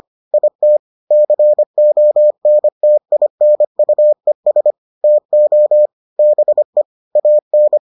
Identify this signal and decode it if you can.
Morse code, heard as a keyed tone, about 25 wpm, 600 Hz. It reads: IT CONTINUES TO BE AN